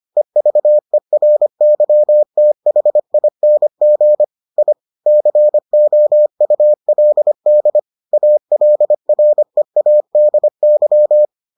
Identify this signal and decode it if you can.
Morse code, heard as a keyed tone, about 25 wpm, 600 Hz. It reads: EVERYTHING I COULD ALREADY